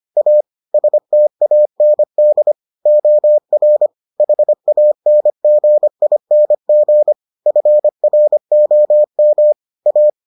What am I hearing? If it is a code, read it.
Morse code, heard as a keyed tone, about 25 wpm, 600 Hz. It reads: A STAND OR HANGING FROM A